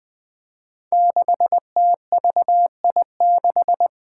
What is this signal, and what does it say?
Morse code, heard as a keyed tone, about 20 wpm, 700 Hz. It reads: 6TVI6
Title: Five Challenging Letter-Number Combinations 20wpm